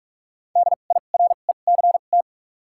Morse code, heard as a keyed tone, about 46 wpm, spaced as for 33 wpm, 700 Hz. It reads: DIRECT
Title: 2x Character Spacing - Top 500 Words - 45wpm